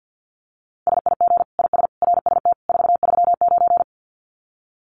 Morse code, heard as a keyed tone, about 50 wpm, 700 Hz. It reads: SIG IS RST 439
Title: Common QSO Phrases 50wpm